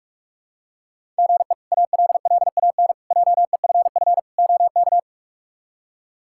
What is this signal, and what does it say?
Morse code, heard as a keyed tone, about 42 wpm, 700 Hz. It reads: GE ALLAN JEFF OK
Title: K1USN SST QSO Practice - 45wpm